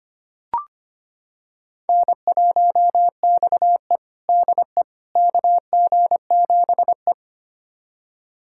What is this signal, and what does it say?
Morse code, heard as a keyed tone, about 25 wpm, 700 Hz. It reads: N1XE DE KG7E
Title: USA Callsign DE Callsign - 4 Characters - 25wpm